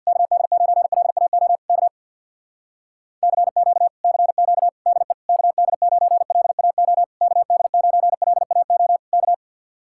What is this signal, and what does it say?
Morse code, heard as a keyed tone, about 63 wpm, 700 Hz. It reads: KD9LAO K CQ CQ DE KD9LAO KD9LAO K